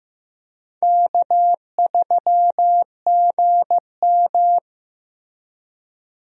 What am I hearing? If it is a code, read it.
Morse code, heard as a keyed tone, about 15 wpm, 700 Hz. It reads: K3GM
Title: USA Call Signs - 4 Characters - Repeated 2x - 15wpm